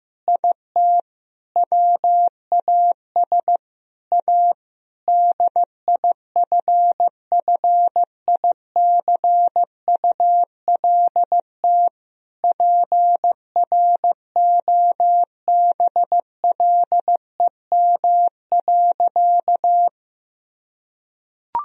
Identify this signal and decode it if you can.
Morse code, heard as a keyed tone, about 15 wpm, 700 Hz. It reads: IT WAS A DIFFICULT PROBLEM.
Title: Sentences from Top 1000 Words - Review Part 2 - 15wpm